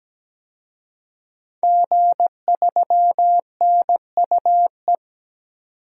Morse code, heard as a keyed tone, about 17 wpm, 700 Hz. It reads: G3NUE
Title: DXCC-100 Call Signs - 5 Characters - 17wpm